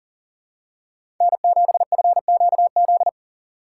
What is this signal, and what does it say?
Morse code, heard as a keyed tone, about 40 wpm, 700 Hz. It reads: N7FQZ
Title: USA Call Signs - 5 Characters - 3x Character Spacing - 40wpm